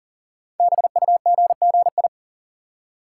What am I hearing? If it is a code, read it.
Morse code, heard as a keyed tone, about 39 wpm, 690 Hz. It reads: BUGGI
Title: Code Groups - 5 Letter:Numbers - 40wpm